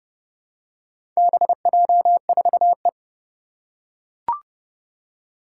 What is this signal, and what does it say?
Morse code, heard as a keyed tone, about 30 wpm, 700 Hz. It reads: BJ4E
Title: Four Challenging Letter-Number Combinations 30wpm